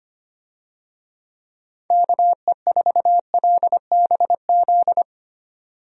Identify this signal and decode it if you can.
Morse code, heard as a keyed tone, about 25 wpm, 700 Hz. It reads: KE4LBZ
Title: USA Call Signs - 6 Characters - 25wpm